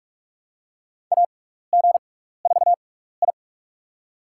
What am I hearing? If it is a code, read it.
Morse code, heard as a keyed tone, about 46 wpm, spaced as for 16 wpm, 700 Hz. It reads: AG4I
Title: USA Call Signs - 4 Characters - 6x Character Spacing - 45wpm